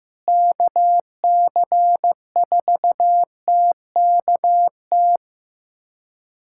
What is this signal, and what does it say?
Morse code, heard as a keyed tone, about 16 wpm, 700 Hz. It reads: KC4TKT
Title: USA Callsign DE Callsign - 6 Characters - 15wpm